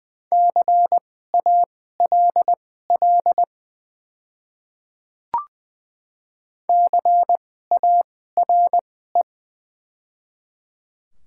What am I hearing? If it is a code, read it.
Morse code, heard as a keyed tone, about 20 wpm, spaced as for 15 wpm, 700 Hz. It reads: CALL CARE
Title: 2x Character Spacing - Top 300 Words - 20wpm